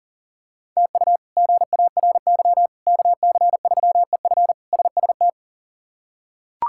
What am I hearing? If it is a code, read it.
Morse code, heard as a keyed tone, about 40 wpm, 700 Hz. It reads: TU GARY KC3EF SST
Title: K1USN SST QSO Practice - 40wpm